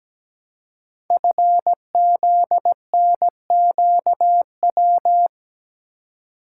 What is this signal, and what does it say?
Morse code, heard as a keyed tone, about 17 wpm, 700 Hz. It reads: FZNQW
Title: Code Groups - 5 Letters - 17wpm